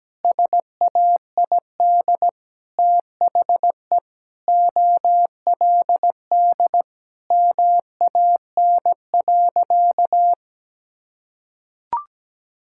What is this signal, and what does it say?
Morse code, heard as a keyed tone, about 17 wpm, 700 Hz. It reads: SAID THE OLD MAN.
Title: Sentences from Top 1000 Words - Review Part 6 - 17wpm